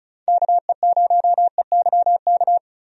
Morse code, heard as a keyed tone, about 36 wpm, 700 Hz. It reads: KE0EYK
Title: USA Call Signs - 6 Characters - Repeated 3x - 35wpm